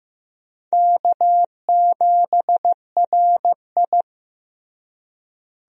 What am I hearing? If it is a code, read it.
Morse code, heard as a keyed tone, about 15 wpm, 700 Hz. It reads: K7RI
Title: USA Call Signs - 4 Characters - Repeated 3x - 15wpm